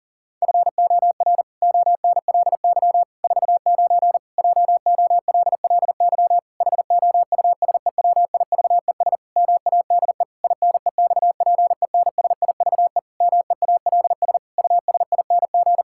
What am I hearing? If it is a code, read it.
Morse code, heard as a keyed tone, about 40 wpm, 700 Hz. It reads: FOR ONLY 49 JOLLY HOUSEWIVES MADE INEXPENSIVE MEALS USING
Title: Pangram Challenge v2 - 40wpm